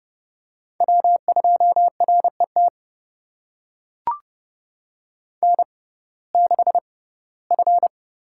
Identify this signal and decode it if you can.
Morse code, heard as a keyed tone, about 30 wpm, spaced as for 18 wpm, 700 Hz. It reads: W2RET N6F
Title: USA Call Signs - 5 Characters - 6x Character Spacing - 30wpm